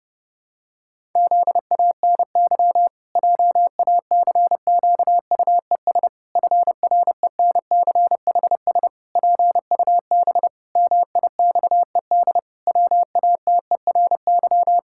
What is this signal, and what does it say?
Morse code, heard as a keyed tone, about 29 wpm, 700 Hz. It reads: ZANY JACQUES FRENCHS PUB MIXED WATERY
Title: Pangram Challenge v2 - 30wpm